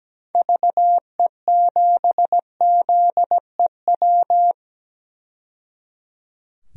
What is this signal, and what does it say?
Morse code, heard as a keyed tone, about 17 wpm, 700 Hz. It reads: VE7ZEW
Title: DXCC-100 Call Signs - 6 Characters - 17wpm